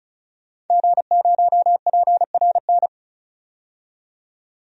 Morse code, heard as a keyed tone, about 36 wpm, 700 Hz. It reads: G0PRN
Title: DXCC-100 Call Signs - 5 Characters - 35wpm